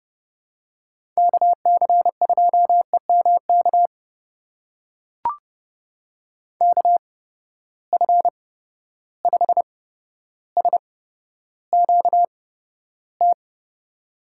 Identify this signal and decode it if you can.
Morse code, heard as a keyed tone, about 30 wpm, spaced as for 13 wpm, 700 Hz. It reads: KC2EMK KF5SQT
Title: USA Call Signs - 6 Characters - 8x Character Spacing - 30wpm